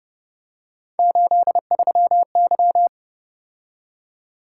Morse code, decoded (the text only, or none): 83Y